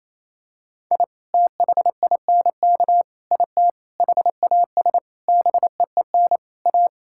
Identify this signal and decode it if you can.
Morse code, heard as a keyed tone, about 28 wpm, 700 Hz. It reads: I THINK IT HAS BEEN A
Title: Sentences from Top 1000 Words - Review Part 6 - 28wpm